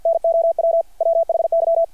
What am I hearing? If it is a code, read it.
Morse code, heard as a keyed tone, about 53 wpm, 660 Hz. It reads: NOW WHY
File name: archived / CRSnet